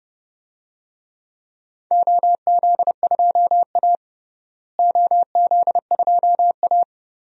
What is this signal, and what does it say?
Morse code, heard as a keyed tone, about 32 wpm, 700 Hz. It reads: OZ2A OZ2A
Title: DXCC-100 Call Signs - 4 Characters - Repeated 3x - 30wpm